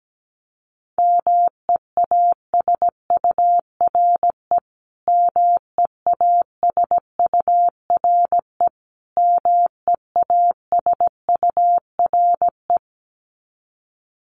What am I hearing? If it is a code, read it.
Morse code, heard as a keyed tone, about 17 wpm, 700 Hz. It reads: MEASURE MEASURE MEASURE
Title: Top 500 Words Repeated 17wpm